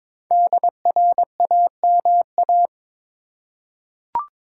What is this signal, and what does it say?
Morse code, heard as a keyed tone, about 22 wpm, 700 Hz. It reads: DRAMA
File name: 5 Letter Words - First 3 Letters Given - 22wpm